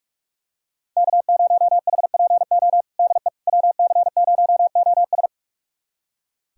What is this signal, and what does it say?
Morse code, heard as a keyed tone, about 46 wpm, 700 Hz. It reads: K0HPO DE WC0OS
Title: USA Callsign DE Callsign - 5 Characters - 45wpm